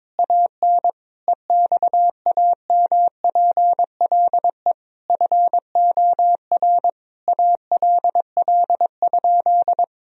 Morse code, heard as a keyed tone, about 22 wpm, 700 Hz. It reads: AN EXAMPLE FOR ALL?
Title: Sentences from Top 1000 Words - Review Part 3 - 22wpm